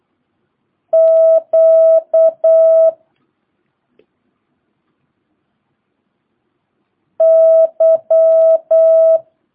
Morse code, heard as a keyed tone, about 8 wpm, 650 Hz. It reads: Q Y
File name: jh8cbh / 2022cw